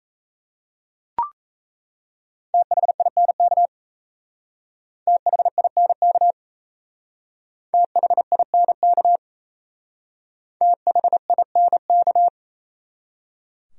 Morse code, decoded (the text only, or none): THINK THINK THINK THINK